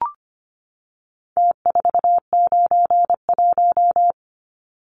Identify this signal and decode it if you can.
Morse code, heard as a keyed tone, about 25 wpm, 700 Hz. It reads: T491
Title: Numbers - Contest Style - 4-Digit 25wpm